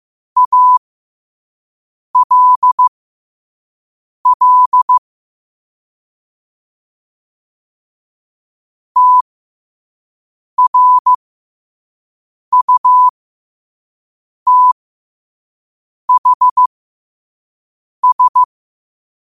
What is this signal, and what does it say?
Morse code, heard as a keyed tone, about 15 wpm, spaced as for 5 wpm, 1000 Hz. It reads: ALL TRUTHS
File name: qotdfiles / 5wpm